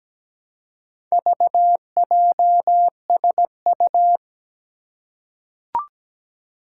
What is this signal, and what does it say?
Morse code, heard as a keyed tone, about 17 wpm, 700 Hz. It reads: VJSU
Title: Four Challenging Letter-Number Combinations 17wpm